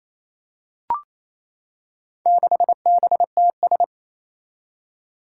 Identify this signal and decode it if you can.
Morse code, heard as a keyed tone, about 29 wpm, 700 Hz. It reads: 6BTS
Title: Four Challenging Letter-Number Combinations 28wpm